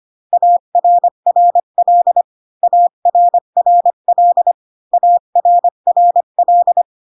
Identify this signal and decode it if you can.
Morse code, heard as a keyed tone, about 25 wpm, 700 Hz. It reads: ARRL ARRL ARRL